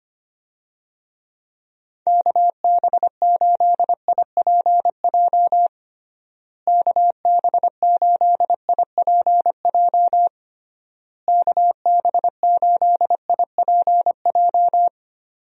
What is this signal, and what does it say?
Morse code, heard as a keyed tone, about 25 wpm, 700 Hz. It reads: KB8IPJ KB8IPJ KB8IPJ
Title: USA Call Signs - 6 Characters - Repeated 3x - 25wpm